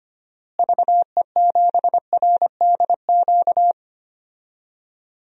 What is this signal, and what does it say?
Morse code, heard as a keyed tone, about 25 wpm, 700 Hz. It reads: VE7RDQ
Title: DXCC-100 Call Signs - 6 Characters - Repeated 3x - 25wpm